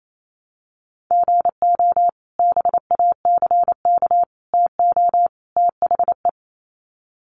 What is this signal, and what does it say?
Morse code, heard as a keyed tone, about 28 wpm, 700 Hz. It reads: GO BACK TO THE
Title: Sets of 4 Words 28wpm